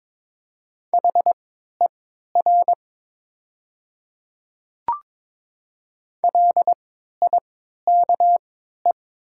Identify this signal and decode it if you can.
Morse code, heard as a keyed tone, about 22 wpm, spaced as for 13 wpm, 700 Hz. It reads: HER LIKE
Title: 3x Character Spacing - Top 100 Words - 22wpm